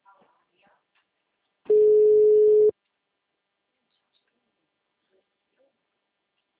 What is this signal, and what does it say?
Ringback tone (Europe, 425 Hz), with speech.